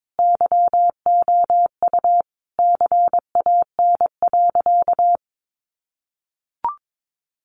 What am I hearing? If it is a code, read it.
Morse code, heard as a keyed tone, about 22 wpm, 700 Hz. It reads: YOU CAN.
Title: Sentences from Top 1100 Words 22wpm